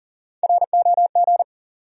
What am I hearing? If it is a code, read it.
Morse code, heard as a keyed tone, about 40 wpm, 700 Hz. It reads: ROG
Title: USA Common Names - Max 3 Letters - 40wpm